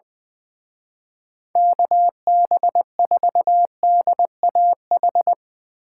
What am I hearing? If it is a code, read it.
Morse code, heard as a keyed tone, about 24 wpm, 700 Hz. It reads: KB4DAH
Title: Speed Racing - USA 2x3 Call Signs 20wpm